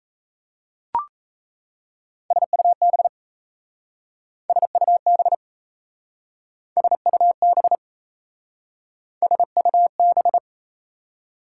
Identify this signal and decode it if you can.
Morse code, heard as a keyed tone, about 36 wpm, 700 Hz. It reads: SUB SUB SUB SUB